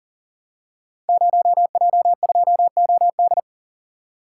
Morse code, heard as a keyed tone, about 41 wpm, 700 Hz. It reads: 0J2OD
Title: Code Groups - 5 Letter:Numbers - 40wpm